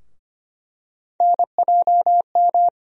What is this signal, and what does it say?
Morse code, heard as a keyed tone, about 25 wpm, 700 Hz. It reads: NJM
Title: USA Call Sign Postfixes - 3 Letters 25wpm